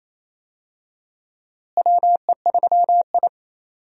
Morse code, heard as a keyed tone, about 28 wpm, 700 Hz. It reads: WE3I
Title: USA Call Signs - 4 Characters - Repeated 3x - 28wpm